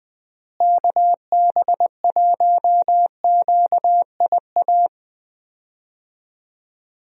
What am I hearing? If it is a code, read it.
Morse code, heard as a keyed tone, about 20 wpm, 700 Hz. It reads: KB1QIA
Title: USA Call Signs - 6 Characters - Repeated 3x - 20wpm